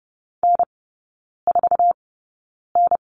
Morse code, heard as a keyed tone, about 30 wpm, spaced as for 9 wpm, 700 Hz. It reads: N4N
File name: Lesson 18e - Letter m - Three Characters 30wpm